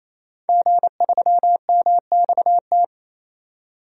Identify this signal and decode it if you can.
Morse code, heard as a keyed tone, about 28 wpm, 700 Hz. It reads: G3MXT
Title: DXCC-100 Call Signs - 5 Characters - 28wpm